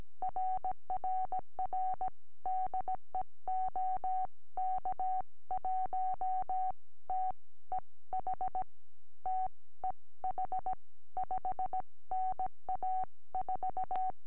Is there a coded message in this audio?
Yes: Morse code from 0.2 to 14.1 s, heard as a keyed tone, about 17 wpm, spaced as for 15 wpm, 750 Hz. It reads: RRR DE OK1TEH TEH 5NA 4